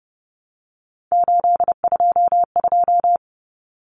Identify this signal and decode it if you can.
Morse code, heard as a keyed tone, about 30 wpm, 700 Hz. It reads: 822
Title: Common Cut Numbers - 3-Digit 30wpm